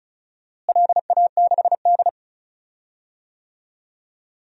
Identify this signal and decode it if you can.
Morse code, heard as a keyed tone, about 36 wpm, 700 Hz. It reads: LA6D
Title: DXCC-100 Call Signs - 4 Characters - Repeated 2x - 35wpm